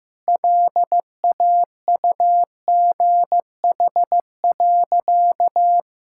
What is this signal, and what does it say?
Morse code, heard as a keyed tone, about 15 wpm, 700 Hz. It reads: LAUGH.
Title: Sentences from Top 1000 Words - Review Part 5 - 15wpm